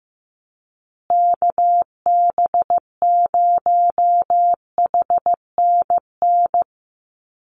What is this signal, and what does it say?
Morse code, heard as a keyed tone, about 15 wpm, 700 Hz. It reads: KB0HNN
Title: USA Call Signs Repeated 2x 15wpm